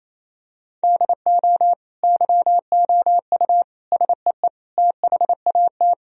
Morse code, heard as a keyed tone, about 28 wpm, 700 Hz. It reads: DO YOU SEE THAT
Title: Sentences from Top 1000 Words - Review Part 3 - 28wpm